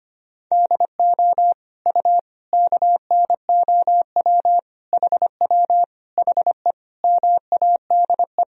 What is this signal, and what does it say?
Morse code, heard as a keyed tone, about 25 wpm, 700 Hz. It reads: DO U KNOW HW HE MADE
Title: Sentences from Top 300 Words and Abbreviations 25wpm